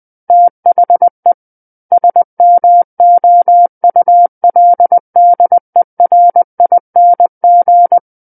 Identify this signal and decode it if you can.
Morse code, heard as a keyed tone, about 20 wpm, 700 Hz. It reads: THE SMOULDERING
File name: 20 WPM